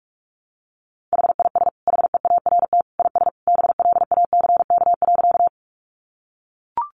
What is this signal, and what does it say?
Morse code, heard as a keyed tone, about 44 wpm, 700 Hz. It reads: HIS HEART IS BLACK.
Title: Sentences from Top 1200 Words 45wpm